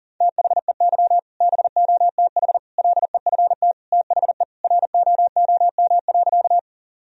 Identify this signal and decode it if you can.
Morse code, heard as a keyed tone, about 40 wpm, 700 Hz. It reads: THEY BOTH LEFT THE ROOM.
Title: Sentences from Top 1000 Words - Review Part 1 - 40wpm